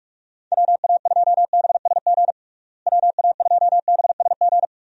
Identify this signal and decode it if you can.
Morse code, heard as a keyed tone, about 41 wpm, 700 Hz. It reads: WA2BSG WA2BSG
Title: USA Call Signs - 6 Characters - Repeated 2x - 45wpm